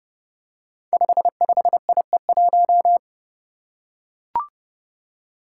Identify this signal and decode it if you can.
Morse code, heard as a keyed tone, about 30 wpm, 700 Hz. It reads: 55IE1
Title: Five Challenging Letter-Number Combinations 30wpm